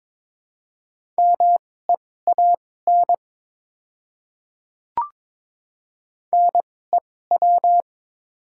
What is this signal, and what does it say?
Morse code, heard as a keyed tone, about 23 wpm, spaced as for 16 wpm, 690 Hz. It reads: MEAN NEW
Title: 2x Character Spacing - Top 200 Words - 22wpm